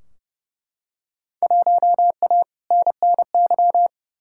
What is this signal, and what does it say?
Morse code, heard as a keyed tone, about 31 wpm, 700 Hz. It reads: 1A NNY
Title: Field Day Exchanges Encore - 30wpm